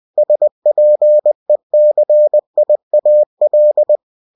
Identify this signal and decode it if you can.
Morse code, heard as a keyed tone, about 20 wpm, 600 Hz. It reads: SPECIAL
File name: cw-books / winnie-the-pooh / winnie-20wpm-page-0112